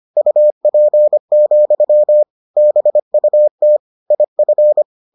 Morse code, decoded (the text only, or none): UP, BUT IF